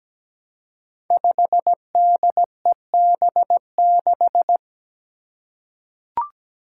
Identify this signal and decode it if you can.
Morse code, heard as a keyed tone, about 17 wpm, 700 Hz. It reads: 5DEB6
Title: Five Challenging Letter-Number Combinations 17wpm